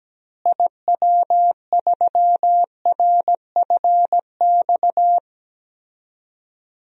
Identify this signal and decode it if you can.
Morse code, heard as a keyed tone, about 17 wpm, 700 Hz. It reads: IW3RFX
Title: DXCC-100 Call Signs - 6 Characters - Repeated 2x - 17wpm